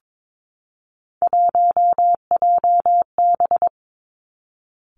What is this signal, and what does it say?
Morse code, heard as a keyed tone, about 22 wpm, 700 Hz. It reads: 1JB
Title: Three Challenging Letter-Number Combinations 22wpm